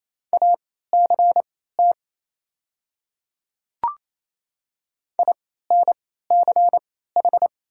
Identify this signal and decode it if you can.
Morse code, heard as a keyed tone, about 28 wpm, spaced as for 16 wpm, 700 Hz. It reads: ACT INCH
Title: 3x Character Spacing - Top 500 Words - 28wpm